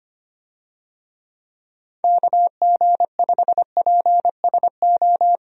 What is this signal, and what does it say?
Morse code, heard as a keyed tone, about 25 wpm, 700 Hz. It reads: KG5PSO
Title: USA Call Signs - 6 Characters - 25wpm